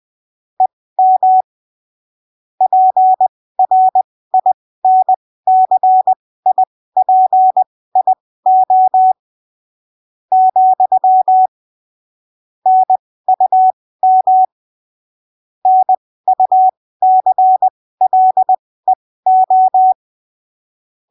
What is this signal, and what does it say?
Morse code, heard as a keyed tone, about 20 wpm, spaced as for 15 wpm, 750 Hz. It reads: EM PRINCIPIO , NUM NUCLEO
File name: arquivos / texto / texto 11 20ppm